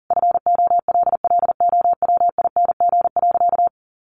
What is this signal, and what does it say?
Morse code, heard as a keyed tone, about 40 wpm, 700 Hz. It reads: FOLLOWING.
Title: Sentences from Top 1300 Words 40wpm